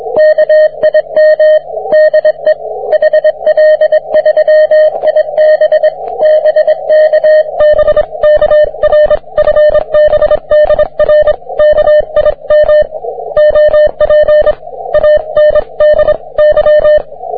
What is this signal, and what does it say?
Morse code, heard as a keyed tone, about 22 wpm, 620 Hz, with radio noise. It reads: KIM DE HL3IB BK BK R FB DR KIM OP ANDY